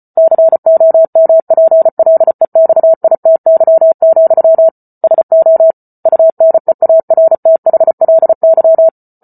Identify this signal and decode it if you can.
Morse code, heard as a keyed tone, about 34 wpm, 650 Hz. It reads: COMPLEXITY, SO UNEARTHLY